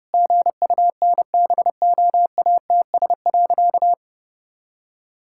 Morse code, heard as a keyed tone, about 32 wpm, 700 Hz. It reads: GUNBOATS.
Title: Pangram Challenge v2 - 30wpm